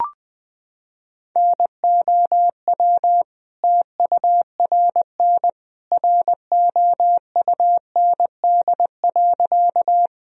Morse code, heard as a keyed tone, about 20 wpm, 700 Hz. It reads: NOW TURN ROUND.
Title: Sentences from Top 1000 Words - Review Part 1 - 20wpm